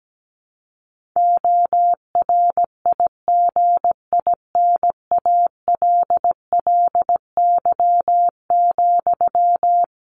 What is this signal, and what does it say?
Morse code, heard as a keyed tone, about 17 wpm, 700 Hz. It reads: ORIGINALLY,
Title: Sentences from Top 1500 Words 17wpm